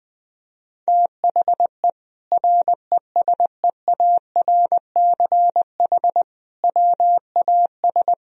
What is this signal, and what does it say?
Morse code, heard as a keyed tone, about 20 wpm, 700 Hz. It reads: THE RESEARCH WAS
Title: Sentences from Top 1000 Words - Review Part 4 - 20wpm